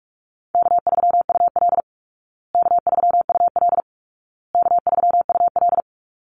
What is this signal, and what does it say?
Morse code, heard as a keyed tone, about 44 wpm, 700 Hz. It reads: K3UL K3UL K3UL
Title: Easy Call Signs Repeated 3x - 45wpm